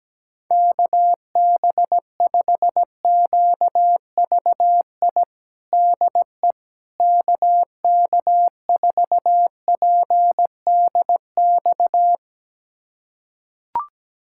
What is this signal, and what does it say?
Morse code, heard as a keyed tone, about 17 wpm, 700 Hz. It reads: KB5QVI DE KK4PDX
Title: USA Callsign DE Callsign - 6 Characters - 17wpm